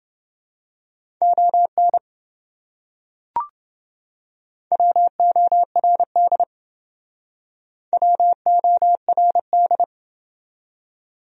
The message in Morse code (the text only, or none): ON WORD WORD